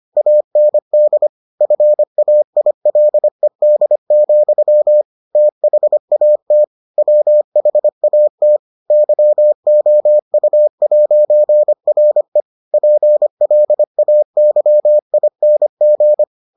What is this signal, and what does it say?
Morse code, heard as a keyed tone, about 25 wpm, 600 Hz. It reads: AND FAILED, THAT WHAT YOU'RE PLAYING